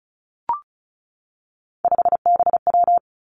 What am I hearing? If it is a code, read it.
Morse code, heard as a keyed tone, about 36 wpm, 700 Hz. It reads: 5BW
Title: Three Challenging Letter-Number Combinations 35wpm